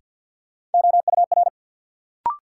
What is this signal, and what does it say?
Morse code, heard as a keyed tone, about 47 wpm, 700 Hz. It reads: OUR
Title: Sentences from Top 200 Words - New Word Drill - 50wpm